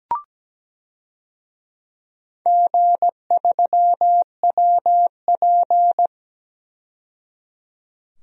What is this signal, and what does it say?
Morse code, heard as a keyed tone, about 17 wpm, 700 Hz. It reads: G3WP
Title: DXCC-100 Call Signs - 4 Characters - 17wpm